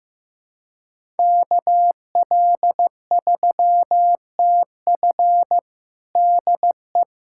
Morse code, heard as a keyed tone, about 15 wpm, 700 Hz. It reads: KL3TF DE
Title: USA Callsign DE Callsign - 5 Characters - 15wpm